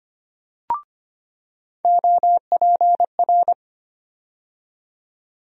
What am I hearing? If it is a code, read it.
Morse code, heard as a keyed tone, about 25 wpm, 700 Hz. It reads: OPR
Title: USA Call Sign Postfixes - 3 Letters 25wpm